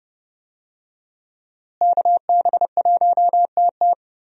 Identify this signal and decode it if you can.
Morse code, heard as a keyed tone, about 30 wpm, 700 Hz. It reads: KB1TT
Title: USA Call Signs - 5 Characters - Repeated 2x - 30wpm